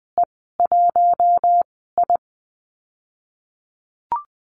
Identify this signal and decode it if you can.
Morse code, heard as a keyed tone, about 20 wpm, 700 Hz. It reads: E 1 I
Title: 062 Lesson 11g - Letter h - Call Signs 20wpm